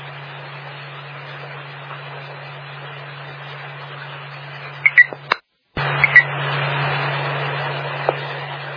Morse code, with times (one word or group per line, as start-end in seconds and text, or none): none